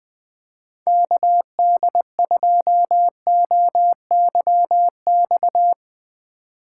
Morse code, heard as a keyed tone, about 21 wpm, 700 Hz. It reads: KD2OYX
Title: Speed Racing - USA Call Signs 17wpm